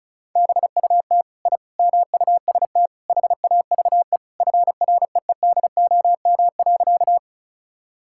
Morse code, heard as a keyed tone, about 34 wpm, 700 Hz. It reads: BUT I MUST HAVE FREEDOM.
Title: Sentences from Top 1000 Words - Review Part 5 - 35wpm